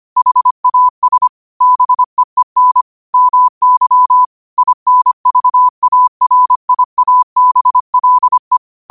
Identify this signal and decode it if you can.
Morse code, heard as a keyed tone, about 25 wpm, 1000 Hz. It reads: HAS BEEN MY INVARIABLE
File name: qotdfiles / 25wpm